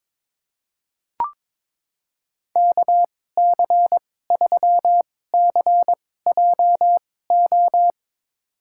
Morse code, heard as a keyed tone, about 22 wpm, spaced as for 16 wpm, 700 Hz. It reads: KC3CJO
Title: USA Call Signs - 2x Character Spacing - 22wpm